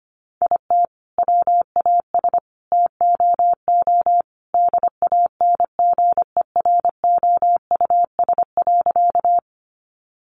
Morse code, heard as a keyed tone, about 25 wpm, 700 Hz. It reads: IT WAS TOO DANGEROUS.
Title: Sentences from Top 1200 Words 25wpm